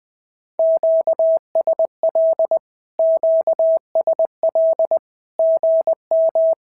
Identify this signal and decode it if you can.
Morse code, heard as a keyed tone, about 20 wpm, 650 Hz. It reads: QSL QSL GM
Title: SOTA QSO Practice 20wpm